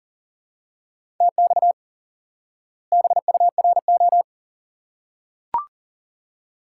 Morse code, heard as a keyed tone, about 37 wpm, 700 Hz. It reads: TX BURO
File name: IQE Course 3 - Lesson 89 - YAGI - Two Elements - 40wpm